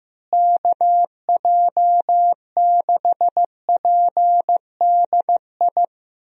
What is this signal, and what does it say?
Morse code, heard as a keyed tone, about 15 wpm, 700 Hz. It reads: KJ6PDI
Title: USA Call Signs - 6 Characters - Repeated 3x - 15wpm